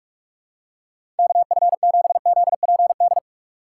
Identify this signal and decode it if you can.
Morse code, heard as a keyed tone, about 46 wpm, 700 Hz. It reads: KF7ZPD
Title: USA Call Signs - 6 Characters - 2x Character Spacing - 45wpm